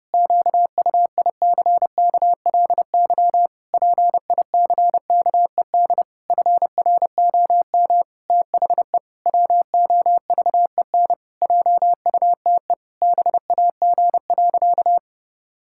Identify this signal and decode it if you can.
Morse code, heard as a keyed tone, about 30 wpm, 700 Hz. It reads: QUICKLY PICKED FROM THE WOVEN JUTE BAG.
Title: Pangram Challenge v2 - 30wpm